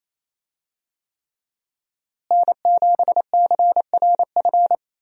Morse code, heard as a keyed tone, about 28 wpm, 700 Hz. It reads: N7CRF
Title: USA Call Signs - 5 Characters - Repeated 3x - 28wpm